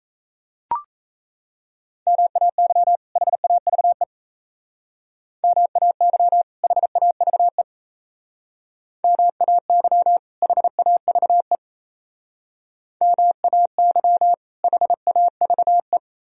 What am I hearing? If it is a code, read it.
Morse code, heard as a keyed tone, about 34 wpm, 700 Hz. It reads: MAY HAVE MAY HAVE MAY HAVE MAY HAVE
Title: Speed-Racing - Sets of 2 Words 28wpm